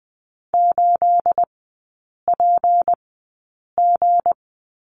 Morse code, heard as a keyed tone, about 20 wpm, spaced as for 8 wpm, 690 Hz. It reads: 8PG
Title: 212 Lesson 34e - Number 8 - Three Characters 20wpm